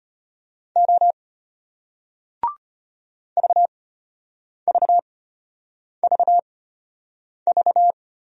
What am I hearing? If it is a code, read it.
Morse code, heard as a keyed tone, about 33 wpm, spaced as for 8 wpm, 700 Hz. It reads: O VVVV